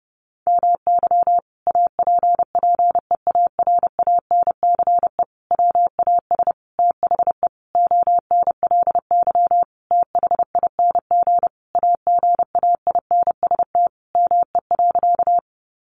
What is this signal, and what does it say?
Morse code, heard as a keyed tone, about 30 wpm, 700 Hz. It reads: MY APPEARANCE WAS THE ONLY THING AGAINST ME.
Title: Sentences from Top 1100 Words 30wpm